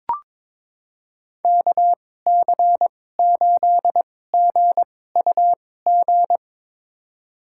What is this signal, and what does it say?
Morse code, heard as a keyed tone, about 21 wpm, spaced as for 16 wpm, 700 Hz. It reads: KC8GUG
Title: USA Call Signs - 6 Characters - 2x Character Spacing - 22wpm